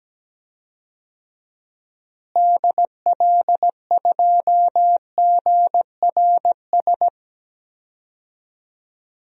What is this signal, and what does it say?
Morse code, heard as a keyed tone, about 17 wpm, 700 Hz. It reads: DL2GRS
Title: DXCC-100 Call Signs - 6 Characters - Repeated 3x - 17wpm